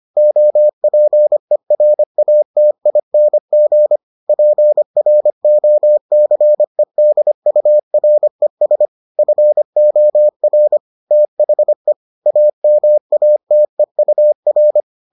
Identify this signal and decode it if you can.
Morse code, heard as a keyed tone, about 25 wpm, 600 Hz. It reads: OPERATING PROCEDURES FOR THE AMATEUR